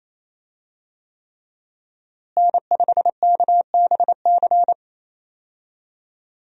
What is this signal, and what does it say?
Morse code, heard as a keyed tone, about 28 wpm, 700 Hz. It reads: N5KBC
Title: USA Call Signs - 5 Characters - Repeated 3x - 28wpm